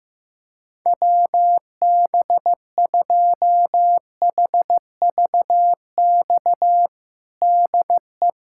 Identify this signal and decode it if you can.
Morse code, heard as a keyed tone, about 16 wpm, 700 Hz. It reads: WB2HVX DE
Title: USA Callsign DE Callsign - 6 Characters - 15wpm